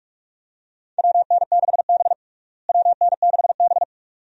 Morse code, heard as a keyed tone, about 42 wpm, 700 Hz. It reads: WN6B WN6B
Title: USA Call Signs - 4 Characters - Repeated 2x - 45wpm